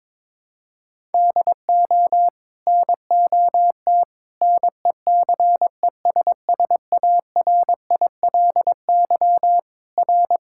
Morse code, heard as a keyed tone, about 22 wpm, 700 Hz. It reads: DO NOT NECESSARILY R
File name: practice / 22wpm / Sets of 8 Words - 22wpm